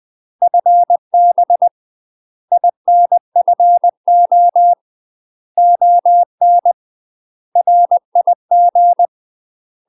Morse code, heard as a keyed tone, about 20 wpm, 700 Hz. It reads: FB INFO ON RIG
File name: UK MASTER QSOs TRAINING/20WPM